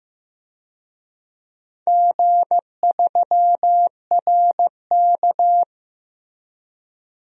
Morse code, heard as a keyed tone, about 15 wpm, 700 Hz. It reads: G3RK
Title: DXCC-100 Call Signs - 4 Characters - 15wpm